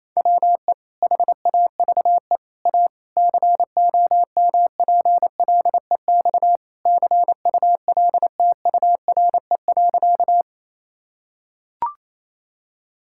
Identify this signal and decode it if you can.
Morse code, heard as a keyed tone, about 27 wpm, 700 Hz. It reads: WE HAVE A COMPLEX CULTURE.
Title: Sentences from Top 1000 Words - Review Part 6 - 28wpm